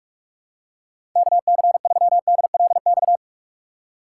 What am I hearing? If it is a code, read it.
Morse code, heard as a keyed tone, about 47 wpm, 700 Hz. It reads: KC3DLX
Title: USA Call Signs - 6 Characters - 4x Character Spacing - 45wpm